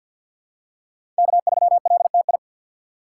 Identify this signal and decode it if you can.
Morse code, heard as a keyed tone, about 47 wpm, 700 Hz. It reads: K3LTI
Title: USA Call Signs - 5 Characters - 4x Character Spacing - 50wpm